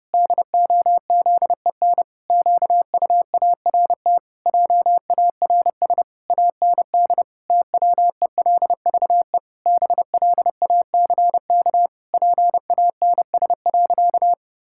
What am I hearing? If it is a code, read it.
Morse code, heard as a keyed tone, about 32 wpm, 700 Hz. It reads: DOZEN QUART JARS AND TWELVE BLACK PANS.
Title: Pangram Challenge v2 - 30wpm